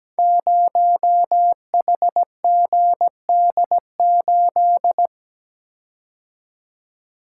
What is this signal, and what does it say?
Morse code, heard as a keyed tone, about 17 wpm, 700 Hz. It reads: 0HGD8
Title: Code Groups - 5 Letter:Numbers - 17wpm